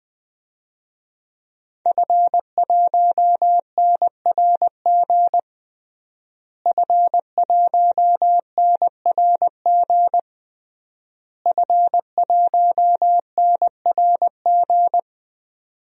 Morse code, heard as a keyed tone, about 20 wpm, 700 Hz. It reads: F1NRG F1NRG F1NRG
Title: DXCC-100 Call Signs - 5 Characters - Repeated 3x - 20wpm